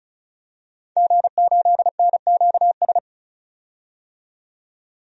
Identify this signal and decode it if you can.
Morse code, heard as a keyed tone, about 34 wpm, 700 Hz. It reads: G8NQS
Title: DXCC-100 Call Signs - 5 Characters - 35wpm